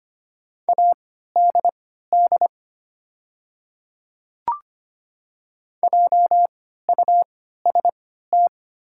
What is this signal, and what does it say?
Morse code, heard as a keyed tone, about 25 wpm, spaced as for 14 wpm, 690 Hz. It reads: ADD JUST
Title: 3x Character Spacing - Top 500 Words - 25wpm